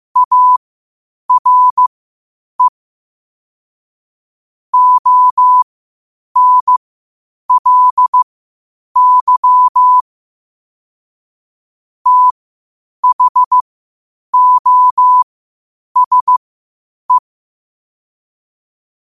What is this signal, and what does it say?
Morse code, heard as a keyed tone, about 15 wpm, spaced as for 8 wpm, 1000 Hz. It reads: ARE ONLY THOSE
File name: qotdfiles / 8wpm